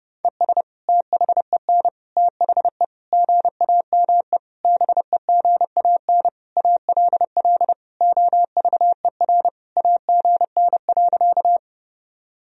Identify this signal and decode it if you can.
Morse code, heard as a keyed tone, about 30 wpm, 700 Hz. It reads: ES THEN THE GAME BEGAN ALL OVER AGN.
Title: Sentences from Top 500 Words and Abbreviations 30wpm